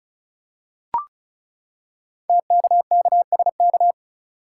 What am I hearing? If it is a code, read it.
Morse code, heard as a keyed tone, about 36 wpm, 690 Hz. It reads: TKKSK
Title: Code Groups - 5 Letters - 35wpm